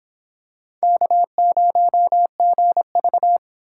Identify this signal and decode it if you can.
Morse code, heard as a keyed tone, about 26 wpm, 700 Hz. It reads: K0GV